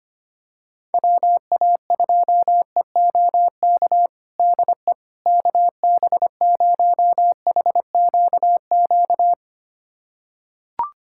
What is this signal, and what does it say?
Morse code, heard as a keyed tone, about 25 wpm, 700 Hz. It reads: WA2EOK DE KB0HQQ
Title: USA Callsign DE Callsign - 6 Characters - 25wpm